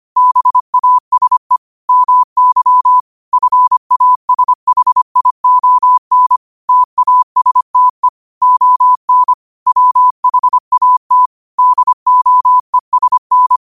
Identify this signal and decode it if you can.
Morse code, heard as a keyed tone, about 25 wpm, 1000 Hz. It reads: BASE MY FASHION TASTE ON WHAT DOESN